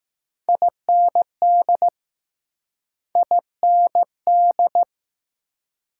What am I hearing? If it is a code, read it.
Morse code, heard as a keyed tone, about 20 wpm, 700 Hz. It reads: IND I N D